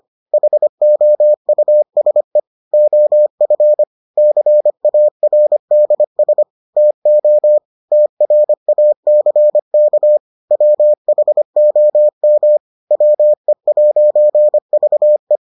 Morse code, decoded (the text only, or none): HOUSE OF CARDS TO TRACK WHOM WE'VE